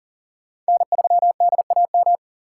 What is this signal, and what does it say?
Morse code, heard as a keyed tone, about 50 wpm, 700 Hz. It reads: N3DAM
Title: Speed Racing - USA 1x3 Call Signs 40wpm